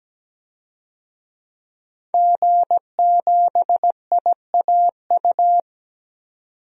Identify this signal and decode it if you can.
Morse code, heard as a keyed tone, about 17 wpm, 700 Hz. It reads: G7IAU
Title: DXCC-100 Call Signs - 5 Characters - 17wpm